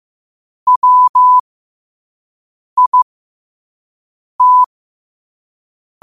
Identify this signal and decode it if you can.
Morse code, heard as a keyed tone, about 15 wpm, spaced as for 5 wpm, 1000 Hz. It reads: WIT
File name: qotdfiles / 5wpm